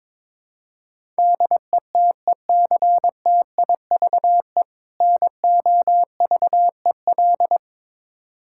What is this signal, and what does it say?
Morse code, heard as a keyed tone, about 22 wpm, 700 Hz. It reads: DETECTIVE NOVEL
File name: Difficult Words with Hints - 22wpm